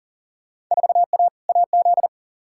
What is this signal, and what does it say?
Morse code, heard as a keyed tone, about 37 wpm, 700 Hz. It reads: 4A AZ
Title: Field Day Exchanges Encore - 40wpm